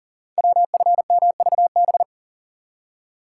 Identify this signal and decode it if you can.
Morse code, heard as a keyed tone, about 41 wpm, 690 Hz. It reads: WFMVB
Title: Code Groups - 5 Letters - 40wpm